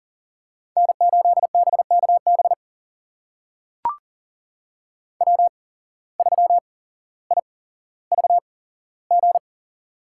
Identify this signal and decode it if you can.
Morse code, heard as a keyed tone, about 40 wpm, spaced as for 17 wpm, 700 Hz. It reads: N8BKB W3IVG
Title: USA Call Signs - 5 Characters - 8x Character Spacing - 40wpm